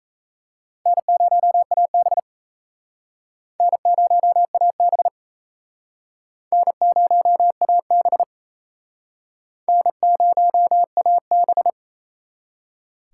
Morse code, heard as a keyed tone, about 38 wpm, 700 Hz. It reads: N0AB N0AB N0AB N0AB